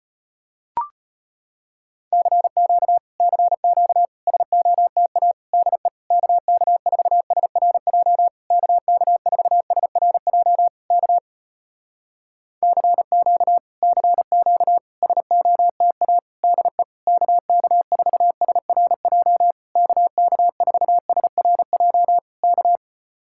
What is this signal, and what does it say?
Morse code, heard as a keyed tone, about 34 wpm, 700 Hz. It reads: CQ CQ SOTA DE KK4SRJ KK4SRJ K CQ CQ SOTA DE KK4SRJ KK4SRJ K